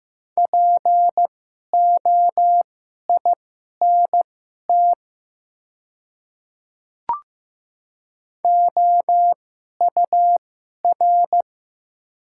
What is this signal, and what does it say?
Morse code, heard as a keyed tone, about 15 wpm, spaced as for 11 wpm, 700 Hz. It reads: POINT OUR
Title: 2x Character Spacing - Top 200 Words - 15wpm